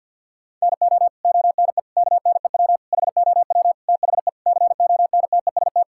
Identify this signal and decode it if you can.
Morse code, heard as a keyed tone, about 50 wpm, 700 Hz. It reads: NO ONE KNEW HOW THE CONTEST
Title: Sentences from Top 1600 Words 50wpm